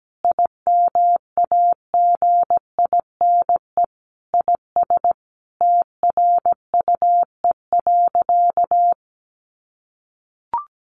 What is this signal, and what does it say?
Morse code, heard as a keyed tone, about 17 wpm, 700 Hz. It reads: IMAGINE IS TRUE.
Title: Sentences from Top 1400 Words 17wpm